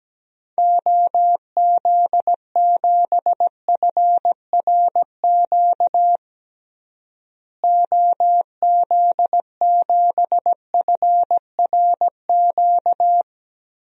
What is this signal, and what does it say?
Morse code, heard as a keyed tone, about 17 wpm, 700 Hz. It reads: OZ7FRQ OZ7FRQ
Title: DXCC-100 Call Signs - 6 Characters - Repeated 3x - 17wpm